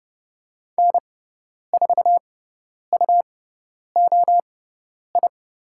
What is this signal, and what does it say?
Morse code, heard as a keyed tone, about 30 wpm, spaced as for 10 wpm, 700 Hz. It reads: N4UOI
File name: Lesson 17g - Letter u - Call Signs 30wpm